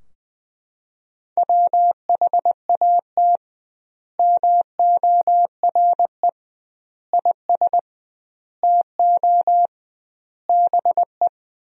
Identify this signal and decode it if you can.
Morse code, heard as a keyed tone, about 20 wpm, 700 Hz. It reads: WHAT MORE IS TO BE
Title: 2x Word-Spacing - Sentences from Top 500 Words - 20wpm